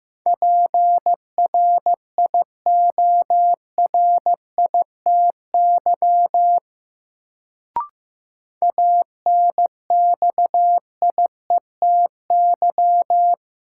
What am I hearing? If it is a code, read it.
Morse code, heard as a keyed tone, about 15 wpm, 700 Hz. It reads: PRIORITY ANXIETY
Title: Sentences from Top 1600 Words - New Word Drill Alt - 15wpm